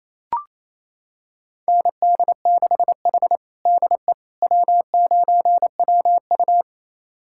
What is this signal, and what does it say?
Morse code, heard as a keyed tone, about 28 wpm, 700 Hz. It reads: ND6H DE W9WU
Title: USA Callsign DE Callsign - 4 Characters - 28wpm